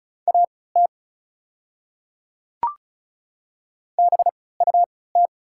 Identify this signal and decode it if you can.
Morse code, heard as a keyed tone, about 36 wpm, spaced as for 20 wpm, 690 Hz. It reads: AT BUT
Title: 3x Character Spacing - Top 100 Words - 35wpm